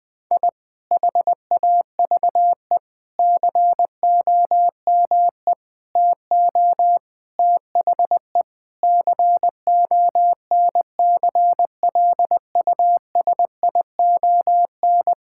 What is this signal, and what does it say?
Morse code, heard as a keyed tone, about 20 wpm, 700 Hz. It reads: I HAVE COME TO THE CONCLUSION
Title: Sentences with 7-Word Phrases - 20wpm